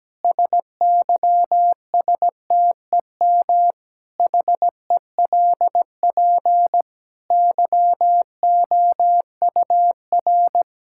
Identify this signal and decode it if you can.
Morse code, heard as a keyed tone, about 17 wpm, 700 Hz. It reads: SYSTEM HELP YOUR
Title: Sentences from Top 1000 Words - Review Part 5 - 17wpm